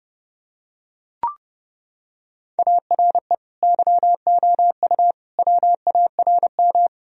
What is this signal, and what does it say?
Morse code, heard as a keyed tone, about 30 wpm, 700 Hz. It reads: ARE YOU WARM
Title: Sentences from Top 1000 Words - Review Part 2 - 30wpm